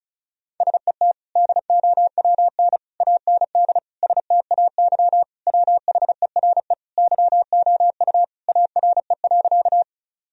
Morse code, heard as a keyed tone, about 36 wpm, 700 Hz. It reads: SET DOWN AND STAY WHERE YOU ARE.
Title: Sentences from Top 1000 Words - Review Part 1 - 35wpm